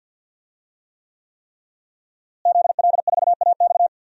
Morse code, heard as a keyed tone, about 48 wpm, 700 Hz. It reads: ZL4AX
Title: DXCC-100 Call Signs - 5 Characters - Repeated 3x - 50wpm